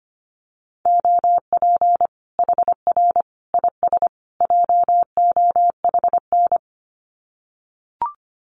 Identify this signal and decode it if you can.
Morse code, heard as a keyed tone, about 25 wpm, 700 Hz. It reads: OP HR IS JOHN
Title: Common QSO Phrases 25wpm